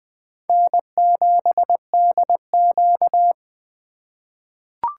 Morse code, heard as a keyed tone, about 20 wpm, 700 Hz. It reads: N7DQ
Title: USA Call Signs - 4 Characters - 2x Character Spacing - 20wpm